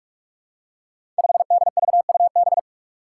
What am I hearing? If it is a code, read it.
Morse code, heard as a keyed tone, about 46 wpm, 700 Hz. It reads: 5DVUB
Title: Five Challenging Letter-Number Combinations 45wpm